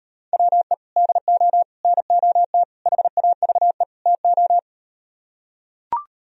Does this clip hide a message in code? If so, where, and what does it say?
Yes: Morse code from 0.3 to 4.6 s, heard as a keyed tone, about 39 wpm, 700 Hz. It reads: WE DO NOT HAVE TO